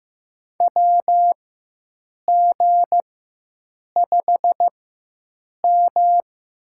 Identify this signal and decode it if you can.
Morse code, heard as a keyed tone, about 15 wpm, spaced as for 7 wpm, 700 Hz. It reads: WG5M
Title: USA Call Signs - 4x Character Spacing - 15wpm